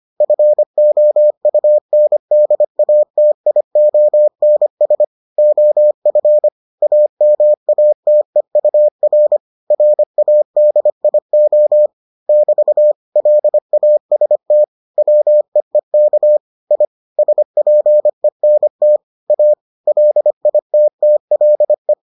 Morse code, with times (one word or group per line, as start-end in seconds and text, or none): 0.2-5.0 FOUNDATIONS
5.4-6.5 OF
6.8-9.4 AMATEUR
9.7-11.9 RADIO
12.3-12.9 <BT>
13.2-14.6 LAST
15.0-16.4 WEEK
16.7-16.9 I
17.2-19.0 SPENT
19.3-19.5 A
19.9-21.9 LITTLE